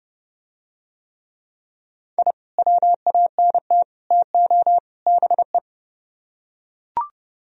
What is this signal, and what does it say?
Morse code, heard as a keyed tone, about 30 wpm, 700 Hz. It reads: I WANT TO BE